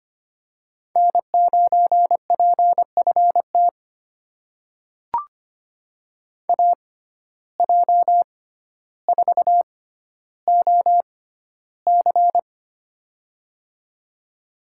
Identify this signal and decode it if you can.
Morse code, heard as a keyed tone, about 25 wpm, spaced as for 13 wpm, 700 Hz. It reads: N9PFT AJ4OC
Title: USA Call Signs - 5 Characters - 6x Character Spacing - 25wpm